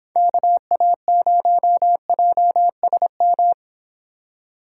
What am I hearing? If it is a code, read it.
Morse code, heard as a keyed tone, about 26 wpm, 700 Hz. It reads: KA0JSM